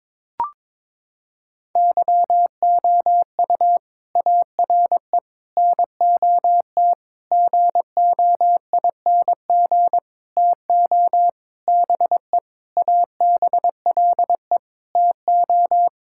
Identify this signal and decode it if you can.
Morse code, heard as a keyed tone, about 22 wpm, 700 Hz. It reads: YOU ARE NOT GOING TO BE ABLE TO
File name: Sentences with 7-Word Phrases - 22wpm